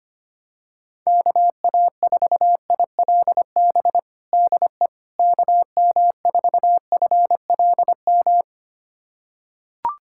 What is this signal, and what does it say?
Morse code, heard as a keyed tone, about 25 wpm, 700 Hz. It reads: KA4ILB DE KM4FLM
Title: USA Callsign DE Callsign 25wpm